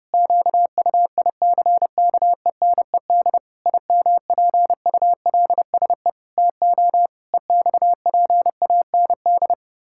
Morse code, heard as a keyed tone, about 30 wpm, 700 Hz. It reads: QUICKENED IMPULSE TO EXPAND
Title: Pangram Challenge v2 - 30wpm